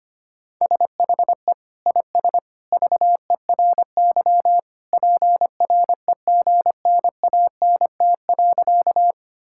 Morse code, heard as a keyed tone, about 25 wpm, 700 Hz. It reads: SHE IS VERY PREGNANT.
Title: Sentences from Top 1000 Words - Review Part 6 - 25wpm